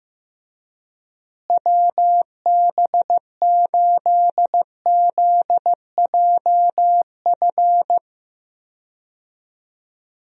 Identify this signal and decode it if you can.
Morse code, heard as a keyed tone, about 16 wpm, 700 Hz. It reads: WB8ZJF
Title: USA Call Signs - 6 Characters - Repeated 3x - 15wpm